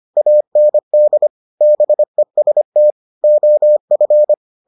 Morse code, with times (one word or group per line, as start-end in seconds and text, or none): 0.2-1.3 AND
1.6-2.9 BEST
3.2-4.3 OF